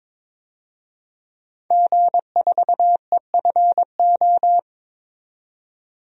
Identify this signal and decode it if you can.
Morse code, heard as a keyed tone, about 21 wpm, 700 Hz. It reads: G4EFO
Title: DXCC-100 Call Signs - 5 Characters - 22wpm